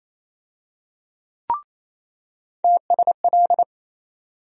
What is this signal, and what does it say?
Morse code, heard as a keyed tone, about 28 wpm, 700 Hz. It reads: TSL
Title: USA Call Sign Postfixes - 3 Letters 28wpm